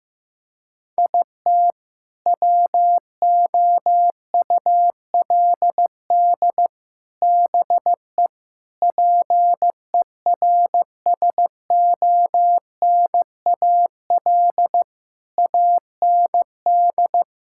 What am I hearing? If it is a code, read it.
Morse code, heard as a keyed tone, about 15 wpm, 700 Hz. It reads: IT WOULD BE PERSONAL AND
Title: Sentences from Top 1000 Words - Review Part 5 - 15wpm